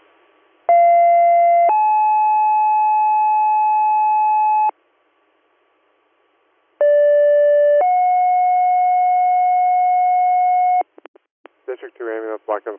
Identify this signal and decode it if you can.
Two-tone page, heard as a one-second tone, then a three-second tone, about 690 then 860 Hz, about 580 then 730 Hz, with speech. It is heard from 0.7 to 10.8 s.